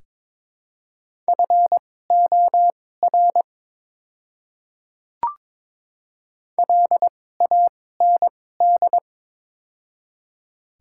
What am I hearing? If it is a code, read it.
Morse code, heard as a keyed tone, about 22 wpm, spaced as for 16 wpm, 700 Hz. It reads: FOR LAND
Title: 2x Character Spacing - Top 200 Words - 22wpm